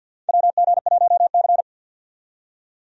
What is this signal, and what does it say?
Morse code, heard as a keyed tone, about 52 wpm, 700 Hz. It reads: WG1C
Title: USA Call Signs - 4 Characters - 50wpm